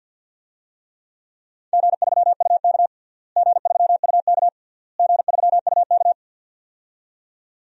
Morse code, heard as a keyed tone, about 50 wpm, 700 Hz. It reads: G3UK G3UK G3UK
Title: DXCC-100 Call Signs - 4 Characters - Repeated 3x - 50wpm